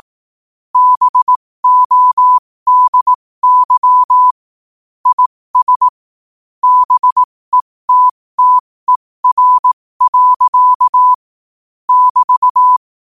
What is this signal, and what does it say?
Morse code, heard as a keyed tone, about 18 wpm, spaced as for 15 wpm, 1000 Hz. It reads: BODY IS BETTER. <BT>
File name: qotdfiles / 15wpm